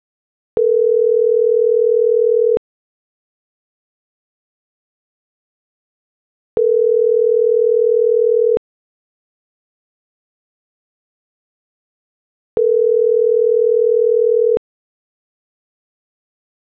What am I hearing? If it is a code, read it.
Ringback tone (North America), 440 + 480 Hz.